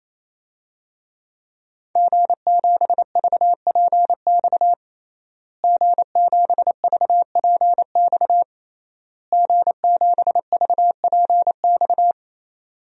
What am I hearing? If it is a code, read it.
Morse code, heard as a keyed tone, about 28 wpm, 700 Hz. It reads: G7VPX G7VPX G7VPX
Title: DXCC-100 Call Signs - 5 Characters - Repeated 3x - 28wpm